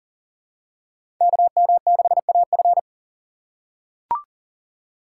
Morse code, heard as a keyed tone, about 40 wpm, 700 Hz. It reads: KM6AF
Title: USA Call Signs - 5 Characters - 4x Character Spacing - 40wpm